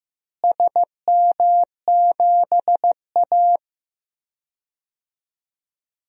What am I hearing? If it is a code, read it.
Morse code, heard as a keyed tone, about 15 wpm, 700 Hz. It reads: SM7A
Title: DXCC-100 Call Signs - 4 Characters - Repeated 3x - 15wpm